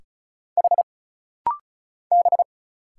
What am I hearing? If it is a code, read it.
Morse code, heard as a keyed tone, about 35 wpm, 700 Hz. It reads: H B
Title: Single Letters - Rapid-Fire 35wpm